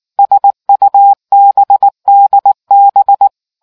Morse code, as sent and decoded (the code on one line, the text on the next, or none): ... ..- -... -.. -...
SUBDB